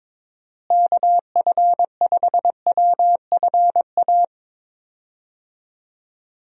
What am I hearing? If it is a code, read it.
Morse code, heard as a keyed tone, about 22 wpm, 700 Hz. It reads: KF5WFA
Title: USA Call Signs - 6 Characters - Repeated 2x - 22wpm